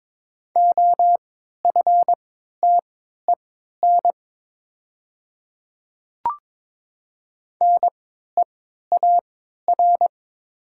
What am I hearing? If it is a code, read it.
Morse code, heard as a keyed tone, about 22 wpm, spaced as for 12 wpm, 700 Hz. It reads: OFTEN NEAR
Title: 3x Character Spacing - Top 300 Words - 22wpm